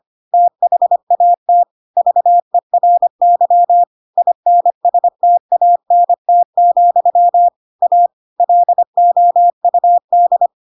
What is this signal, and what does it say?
Morse code, heard as a keyed tone, about 25 wpm, 700 Hz. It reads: THAT VERY INSTANT, A LOUD